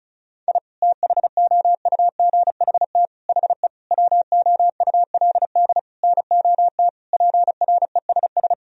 Morse code, heard as a keyed tone, about 34 wpm, 700 Hz. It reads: I THOUGHT HE WOULD NOT PRESS
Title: Sentences from Top 1000 Words - Review Part 1 - 35wpm